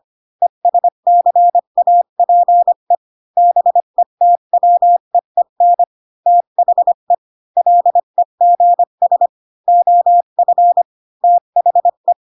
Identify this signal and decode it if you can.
Morse code, heard as a keyed tone, about 25 wpm, 700 Hz. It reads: ESCAPE BETWEEN THE LEGS OF THE
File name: Pinocchio / Pinocchio ch3 20 25 ik1hge